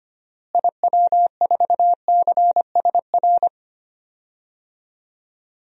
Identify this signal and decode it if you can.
Morse code, heard as a keyed tone, about 25 wpm, 700 Hz. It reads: IW4CSR
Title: DXCC-100 Call Signs - 6 Characters - Repeated 3x - 25wpm